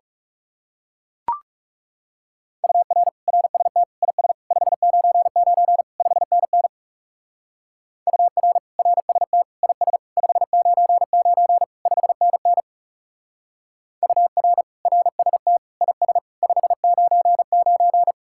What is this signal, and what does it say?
Morse code, heard as a keyed tone, about 37 wpm, 700 Hz. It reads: UR RST IS 599 5NN UR RST IS 599 5NN UR RST IS 599
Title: Speed-Racing Common QSO Phrases - 30wpm